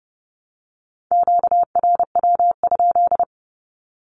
Morse code, heard as a keyed tone, about 30 wpm, 700 Hz. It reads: QRW?
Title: Q-Codes Repeated 3x 30wpm